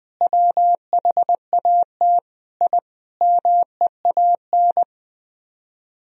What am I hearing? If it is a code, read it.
Morse code, heard as a keyed tone, about 20 wpm, 700 Hz. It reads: WHAT I MEAN
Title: Speed-Racing - Sets of 5 Words 20wpm